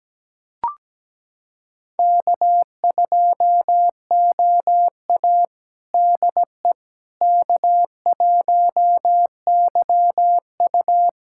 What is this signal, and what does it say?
Morse code, heard as a keyed tone, about 17 wpm, 700 Hz. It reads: K2OA DE K1YU
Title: USA Callsign DE Callsign - 4 Characters - 17wpm